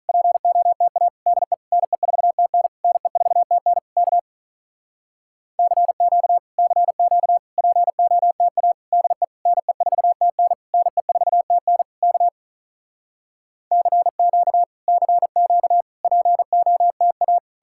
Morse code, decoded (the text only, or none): POTA DE NE4TN NE4TN K CQ CQ POTA DE NE4TN NE4TN K CQ CQ POTA